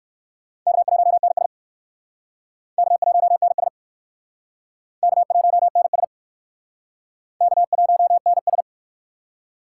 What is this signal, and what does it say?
Morse code, heard as a keyed tone, about 55 wpm, 700 Hz. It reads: K1NS K1NS K1NS K1NS
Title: Speed Racing - USA 1x2 Call Signs 45wpm